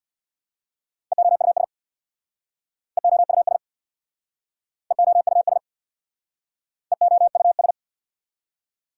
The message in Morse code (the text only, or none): EOUS EOUS EOUS EOUS